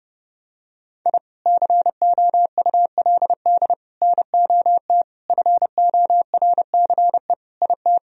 Morse code, heard as a keyed tone, about 32 wpm, 700 Hz. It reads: I COULD NOT FORCE IT
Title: Sentences from Top 1000 Words - Review Part 3 - 30wpm